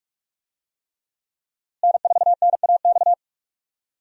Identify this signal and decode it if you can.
Morse code, heard as a keyed tone, about 44 wpm, 700 Hz. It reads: N4NAX
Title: USA Call Signs - 5 Characters - Repeated 3x - 45wpm